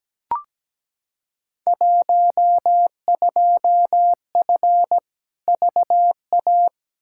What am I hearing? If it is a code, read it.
Morse code, heard as a keyed tone, about 17 wpm, 700 Hz. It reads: 12F VA
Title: Field Day Exchanges Encore - 17wpm